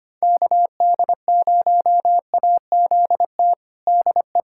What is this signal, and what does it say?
Morse code, heard as a keyed tone, about 25 wpm, 700 Hz. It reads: KD0AZT DE
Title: USA Callsign DE Callsign 25wpm